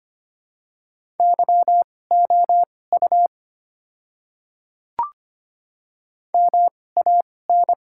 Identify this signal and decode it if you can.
Morse code, heard as a keyed tone, about 25 wpm, spaced as for 18 wpm, 700 Hz. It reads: YOU MAN
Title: 2x Character Spacing - Top 200 Words - 25wpm